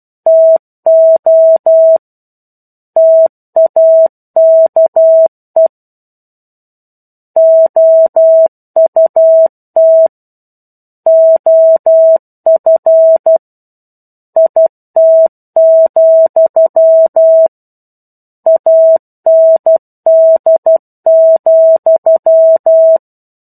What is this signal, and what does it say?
Morse code, heard as a keyed tone, about 12 wpm, 650 Hz. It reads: TO TAKE OUT OF IT, AND,
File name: 01-Chapter-01 12 WPM